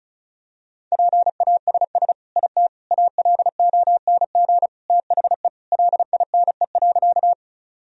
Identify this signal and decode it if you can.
Morse code, heard as a keyed tone, about 36 wpm, 700 Hz. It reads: PASS IT ALONG THE LINE.
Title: Sentences from Top 1000 Words - Review Part 1 - 35wpm